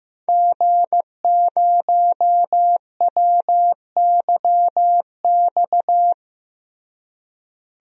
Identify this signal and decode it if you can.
Morse code, heard as a keyed tone, about 16 wpm, 700 Hz. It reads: G0WYX
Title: DXCC-100 Call Signs - 5 Characters - Repeated 2x - 15wpm